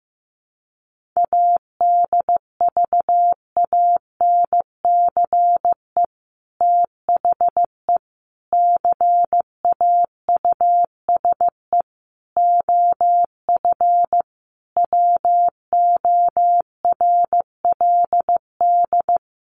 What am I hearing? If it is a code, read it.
Morse code, heard as a keyed tone, about 15 wpm, 700 Hz. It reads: ADVANCE THE CAUSE OF WORLD
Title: Sentences from Top 1400 Words 15wpm